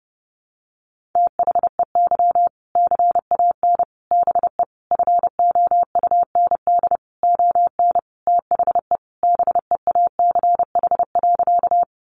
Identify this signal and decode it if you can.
Morse code, heard as a keyed tone, about 28 wpm, 700 Hz. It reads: THEY CAN BE FOUND ON THE BEACH.
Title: Sentences from Top 1500 Words 30wpm